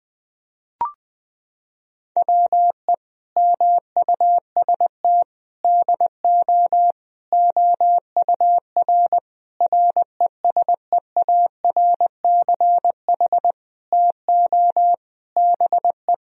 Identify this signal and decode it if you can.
Morse code, heard as a keyed tone, about 19 wpm, 700 Hz. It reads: WE MUST DO OUR RESEARCH TO BE
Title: Sentences from Top 1000 Words - Review Part 4 - 20wpm